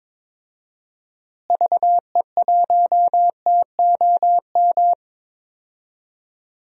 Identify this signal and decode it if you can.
Morse code, heard as a keyed tone, about 21 wpm, 700 Hz. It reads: VE1TOM
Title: DXCC-100 Call Signs - 6 Characters - Repeated 3x - 22wpm